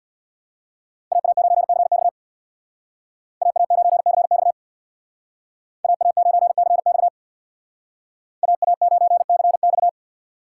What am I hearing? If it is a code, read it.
Morse code, heard as a keyed tone, about 63 wpm, 690 Hz. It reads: AA9CX AA9CX AA9CX AA9CX